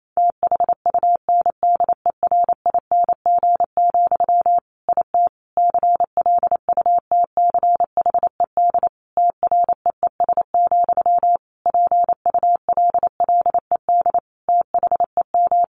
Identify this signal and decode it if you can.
Morse code, heard as a keyed tone, about 28 wpm, 700 Hz. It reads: THUNDERING, IT CLUTCHED TREES, PULLED THEM
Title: A Sound Of Thunder 28wpm